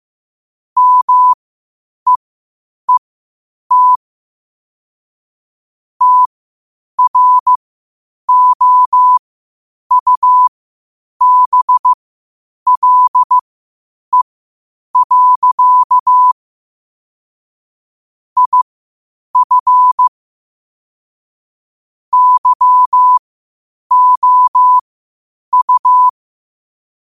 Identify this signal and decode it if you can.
Morse code, heard as a keyed tone, about 15 wpm, spaced as for 8 wpm, 1000 Hz. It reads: MEET TROUBLE. IF YOU
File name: qotdfiles / 8wpm